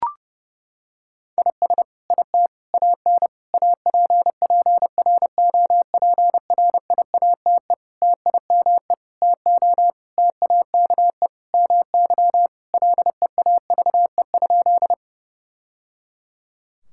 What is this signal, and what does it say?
Morse code, heard as a keyed tone, about 32 wpm, 700 Hz. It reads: IS IT AN APPROPRIATE TIME TO TAKE MY LEAVE?
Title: Sentences from Top 1000 Words - Review Part 5 - 30wpm